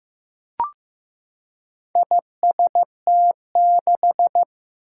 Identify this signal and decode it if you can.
Morse code, heard as a keyed tone, about 15 wpm, 700 Hz. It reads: IST6
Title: Four Challenging Letter-Number Combinations 15wpm